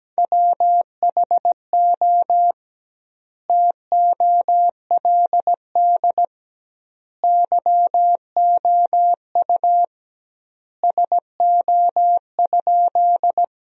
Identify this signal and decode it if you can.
Morse code, heard as a keyed tone, about 17 wpm, 700 Hz. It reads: WHO TOLD YOU SO?
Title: 2x Word-Spacing - Sentences from Top 400 Words - 17wpm